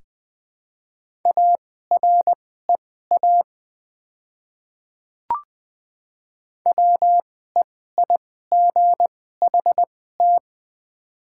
Morse code, heard as a keyed tone, about 20 wpm, spaced as for 14 wpm, 700 Hz. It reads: AREA WEIGHT
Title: 2x Character Spacing - Top 500 Words - 20wpm